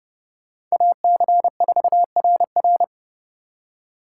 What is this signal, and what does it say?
Morse code, heard as a keyed tone, about 36 wpm, 700 Hz. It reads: AC4RR
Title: Speed Racing - USA Call Signs 30wpm